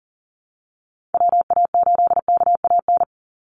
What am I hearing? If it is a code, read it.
Morse code, heard as a keyed tone, about 40 wpm, 700 Hz. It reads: WA8KAN
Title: Call Signs Repeated 3x - 40wpm